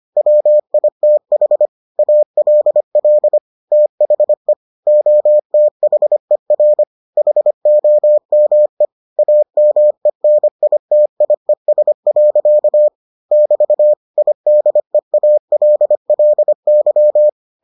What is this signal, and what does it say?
Morse code, heard as a keyed tone, about 25 wpm, 600 Hz. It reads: WITH ALL THE OTHER HOME AMENITIES. <BT> IDEALLY